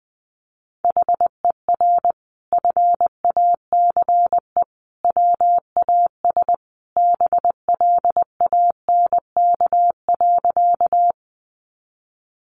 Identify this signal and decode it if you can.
Morse code, heard as a keyed tone, about 20 wpm, 700 Hz. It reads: HER FACE WAS BLANK.
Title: Sentences from Top 1500 Words 20wpm